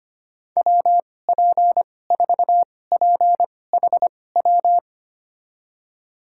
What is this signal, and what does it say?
Morse code, heard as a keyed tone, about 25 wpm, spaced as for 18 wpm, 700 Hz. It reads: WP4PHW
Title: USA Call Signs - 6 Characters - 2x Character Spacing - 25wpm